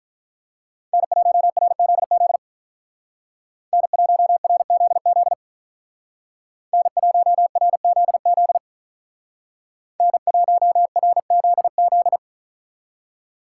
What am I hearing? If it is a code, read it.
Morse code, heard as a keyed tone, about 42 wpm, 700 Hz. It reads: N1RZZ N1RZZ N1RZZ N1RZZ
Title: Speed Racing - USA 1x3 Call Signs 35wpm